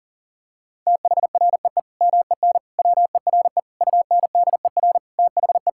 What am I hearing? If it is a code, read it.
Morse code, heard as a keyed tone, about 40 wpm, 700 Hz. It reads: THREE MEN WERE UNDER THE
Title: Sentences from Top 1600 Words 40wpm